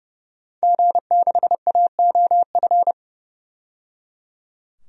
Morse code, heard as a keyed tone, about 30 wpm, 700 Hz. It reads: G6AOF
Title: DXCC-100 Call Signs - 5 Characters - 30wpm